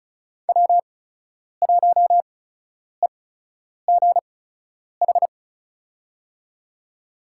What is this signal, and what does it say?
Morse code, heard as a keyed tone, about 34 wpm, spaced as for 10 wpm, 700 Hz. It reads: W1EGH
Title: USA Call Signs - 5 Characters - 8x Character Spacing - 35wpm